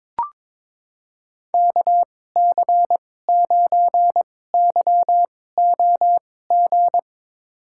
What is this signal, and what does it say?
Morse code, heard as a keyed tone, about 22 wpm, spaced as for 16 wpm, 700 Hz. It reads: KC9YOG
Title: USA Call Signs - 6 Characters - 2x Character Spacing - 22wpm